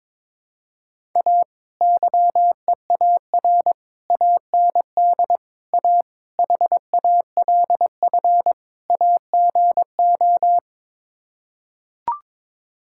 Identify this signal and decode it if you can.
Morse code, heard as a keyed tone, about 21 wpm, 700 Hz. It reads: A YEAR AND A HALF AGO
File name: practice / 22wpm / Sets of 6 Words - 22wpm